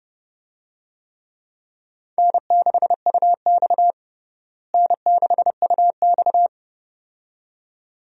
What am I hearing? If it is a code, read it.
Morse code, heard as a keyed tone, about 30 wpm, 700 Hz. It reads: N6UX N6UX
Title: USA Call Signs - 4 Characters - Repeated 2x - 30wpm